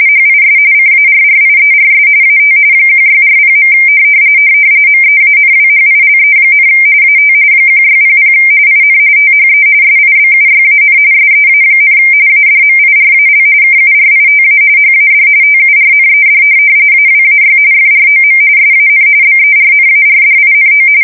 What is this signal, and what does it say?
RTTY, about 45.45 baud. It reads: HIS CARRIERS AT 1640. MEANWHILE, AT 1600 IN THE SIBUYAN SEA, ADMIRAL KURITA HAD TURNED AWAY FROM SAN BERNARDINO
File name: fsk170 / USS-New-Jersey